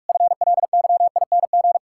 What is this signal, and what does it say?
Morse code, heard as a keyed tone, about 46 wpm, 700 Hz. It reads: FLYING
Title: Pangram Challenge v2 - 45wpm